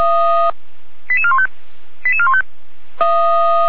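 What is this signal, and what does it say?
Five-tone selective call (ZVEI), heard as quick runs of five tones, tones of about 70 ms. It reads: 88415 88415
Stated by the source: ZVEI five-tone selective call 88337 received by SDR.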